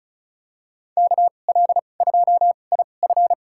Morse code, heard as a keyed tone, about 36 wpm, spaced as for 25 wpm, 700 Hz. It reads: KL2IF
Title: USA Call Signs - 5 Characters - 2x Character Spacing - 35wpm